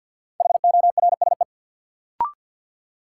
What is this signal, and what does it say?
Morse code, heard as a keyed tone, about 53 wpm, 700 Hz. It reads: HORSE
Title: Sentences from Top 300 Words - New Word Drill - 50wpm